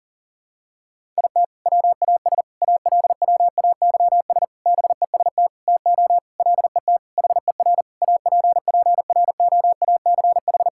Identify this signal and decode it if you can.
Morse code, heard as a keyed tone, about 40 wpm, 700 Hz. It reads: IT WAS ALWAYS BEST TO LET HER APPROACH
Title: Sentences from Top 1000 Words - Review Part 6 - 40wpm